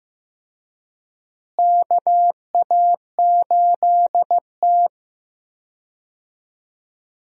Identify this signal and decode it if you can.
Morse code, heard as a keyed tone, about 15 wpm, 700 Hz. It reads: KA8T
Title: USA Call Signs - 4 Characters - Repeated 3x - 15wpm